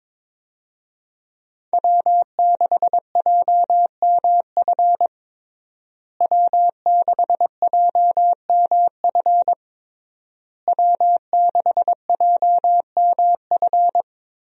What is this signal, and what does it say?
Morse code, heard as a keyed tone, about 22 wpm, 700 Hz. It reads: W6JMF W6JMF W6JMF
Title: USA Call Signs - 5 Characters - Repeated 3x - 22wpm